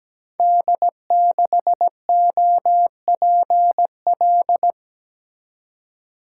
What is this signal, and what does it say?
Morse code, heard as a keyed tone, about 17 wpm, 700 Hz. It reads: D6OPL
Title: Code Groups - 5 Letter:Numbers - 17wpm